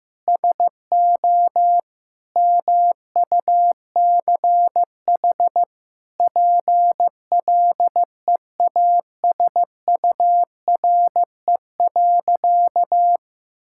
Morse code, heard as a keyed tone, about 15 wpm, 700 Hz. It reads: SO MUCH PLEASURE.
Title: Sentences from Top 1600 Words 15wpm